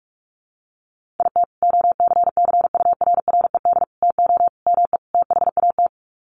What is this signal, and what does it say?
Morse code, heard as a keyed tone, about 44 wpm, 700 Hz. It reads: IT OCCURRED TO ME THAT
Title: Sets of 5 Words Encore 45wpm